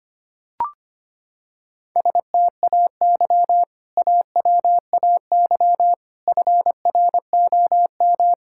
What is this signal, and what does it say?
Morse code, heard as a keyed tone, about 25 wpm, 690 Hz. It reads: STAY AWAY FROM
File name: practice / 25wpm / Sentences with 2-Word Phrases - 25wpm